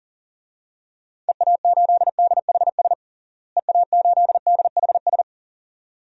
Morse code, heard as a keyed tone, about 40 wpm, 700 Hz. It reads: EA8DHS EA8DHS
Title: DXCC-100 Call Signs - 6 Characters - Repeated 2x - 40wpm